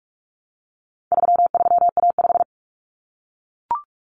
Morse code, heard as a keyed tone, about 44 wpm, 700 Hz. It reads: 33A5
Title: Common Cut Numbers - 4-Digit 45wpm